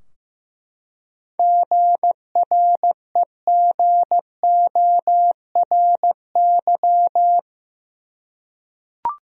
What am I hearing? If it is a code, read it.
Morse code, heard as a keyed tone, about 15 wpm, 700 Hz. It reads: GREGORY
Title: USA Common Names - 15wpm